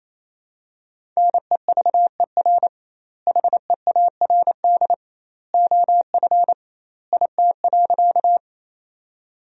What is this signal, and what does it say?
Morse code, heard as a keyed tone, about 27 wpm, 700 Hz. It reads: NEVER HEARD OF IT.
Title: 2x Word-Spacing - Sentences from Top 400 Words - 28wpm